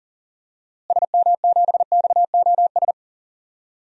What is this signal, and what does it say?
Morse code, heard as a keyed tone, about 40 wpm, 700 Hz. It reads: SM7XOS
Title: DXCC-100 Call Signs - 6 Characters - Repeated 2x - 40wpm